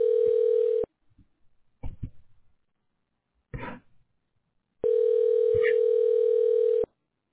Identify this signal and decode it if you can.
Ringback tone (North America), 440 + 480 Hz.